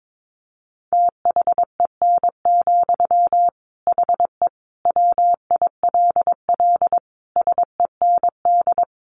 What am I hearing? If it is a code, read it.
Morse code, heard as a keyed tone, about 22 wpm, 700 Hz. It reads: THEN, HE WILL SEND
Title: Sentences from Top 1500 Words 22wpm